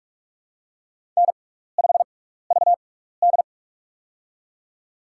Morse code, heard as a keyed tone, about 46 wpm, spaced as for 16 wpm, 700 Hz. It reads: N5VD
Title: USA Call Signs - 4 Characters - 6x Character Spacing - 45wpm